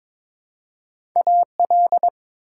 Morse code, heard as a keyed tone, about 22 wpm, 700 Hz. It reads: AL
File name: ARRL Sections v2 - 22wpm